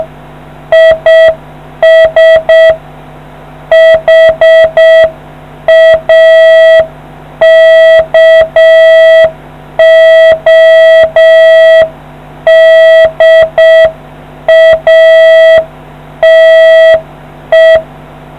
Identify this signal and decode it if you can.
Morse code, heard as a keyed tone, about 6 wpm, 650 Hz. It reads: IS HAKODATE